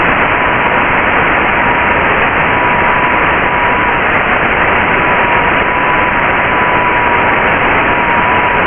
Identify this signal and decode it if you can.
FT8.
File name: Radio / SDR Audio / audio z 19-48-00 11-09-2022